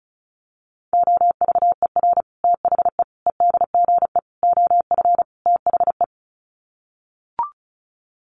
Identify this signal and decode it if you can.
Morse code, heard as a keyed tone, about 34 wpm, 700 Hz. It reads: OVER THE EDGE OF THE
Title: Sets of 5 Words Encore2 35wpm